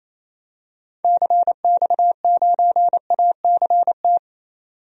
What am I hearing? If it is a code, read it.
Morse code, heard as a keyed tone, about 28 wpm, 700 Hz. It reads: CX9ACT
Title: DXCC-100 Call Signs - 6 Characters - 28wpm